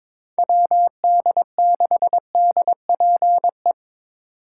Morse code, heard as a keyed tone, about 25 wpm, 700 Hz. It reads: WD6DPE
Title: Speed Racing - USA 2x3 Call Signs 22wpm